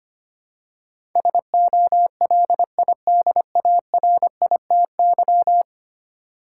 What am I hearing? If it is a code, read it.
Morse code, heard as a keyed tone, about 25 wpm, 700 Hz. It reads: SOLIDARITY
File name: 10 Letter Words - First 3 Letters Given - 25wpm